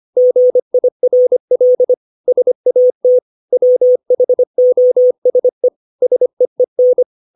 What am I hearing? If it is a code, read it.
Morse code, heard as a keyed tone, about 25 wpm, 500 Hz. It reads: GIRL SAT WHOSE SEEN
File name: 632 words / 632 Words 25WPM